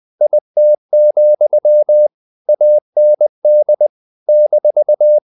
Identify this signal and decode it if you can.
Morse code, heard as a keyed tone, about 20 wpm, 600 Hz. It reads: IT, AND -
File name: CW / Practice / If20